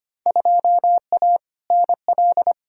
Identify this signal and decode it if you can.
Morse code, heard as a keyed tone, about 25 wpm, 700 Hz. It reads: 2A NL
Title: Field Day Exchanges Encore - 25wpm